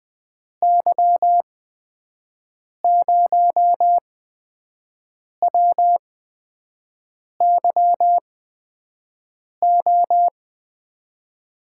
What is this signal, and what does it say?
Morse code, heard as a keyed tone, about 20 wpm, spaced as for 5 wpm, 700 Hz. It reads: Y0WYO
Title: USA Call Signs - 8x Character Spacing - 20wpm